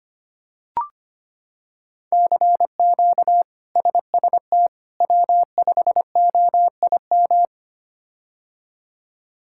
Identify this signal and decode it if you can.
Morse code, heard as a keyed tone, about 25 wpm, 700 Hz. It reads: CQ SST W5OIM
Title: K1USN SST QSO Practice - 25wpm